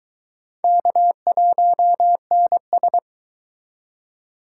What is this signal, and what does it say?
Morse code, heard as a keyed tone, about 22 wpm, 700 Hz. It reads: K1NS